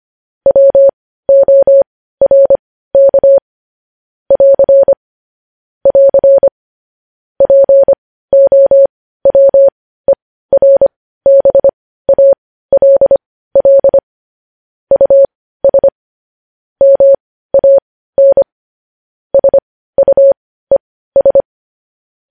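Morse code, heard as a keyed tone, about 25 wpm, spaced as for 15 wpm, 550 Hz. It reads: WORK <AR> <AR> POWERBALL US MAN SUES
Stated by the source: Morse code news headlines sent at 15 words per minute.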